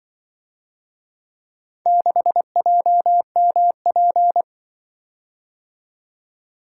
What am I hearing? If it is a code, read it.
Morse code, heard as a keyed tone, about 25 wpm, 700 Hz. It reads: 6JMP
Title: Speed-Racing - Four Letter-Number Combinations - 20wpm